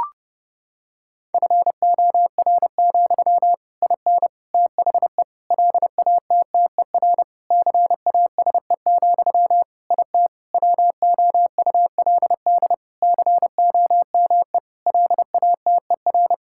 Morse code, heard as a keyed tone, about 32 wpm, 700 Hz. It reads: FOR, IN THE LATTER CASE, IT WOULD COME LATER
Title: Sentences from Top 1600 Words 30wpm